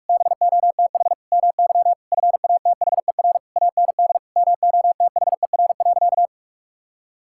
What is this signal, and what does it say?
Morse code, heard as a keyed tone, about 41 wpm, 700 Hz. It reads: BOTH MY FATHER AND MOTHER.
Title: Sentences from Top 1600 Words 45wpm